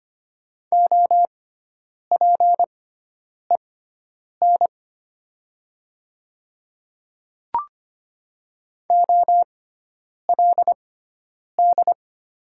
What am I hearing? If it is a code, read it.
Morse code, heard as a keyed tone, about 25 wpm, spaced as for 9 wpm, 700 Hz. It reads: OPEN OLD
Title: 6x Character Spacing - Top 500 Words - 25wpm